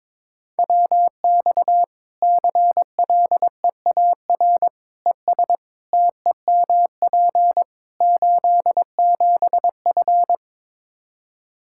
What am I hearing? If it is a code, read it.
Morse code, heard as a keyed tone, about 22 wpm, 700 Hz. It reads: WX CLEAR ES TEMP 87F
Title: Practice QSO - Exchange 2 - Info - 22wpm